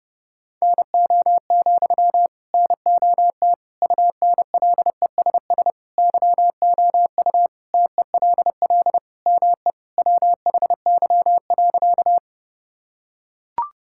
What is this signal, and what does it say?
Morse code, heard as a keyed tone, about 32 wpm, 700 Hz. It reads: NO, NOT UNLESS YOU TELL ME WHY.
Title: Sentences from Top 1000 Words - Review Part 6 - 30wpm